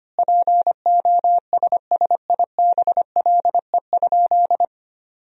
Morse code, heard as a keyed tone, about 25 wpm, 700 Hz. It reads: POSSIBLE?
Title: 4x Word-Spacing - Sentences from Top 500 Words - 25wpm